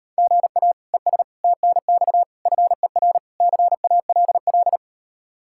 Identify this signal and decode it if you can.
Morse code, heard as a keyed tone, about 39 wpm, 700 Hz. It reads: GA ES TNX FER CALL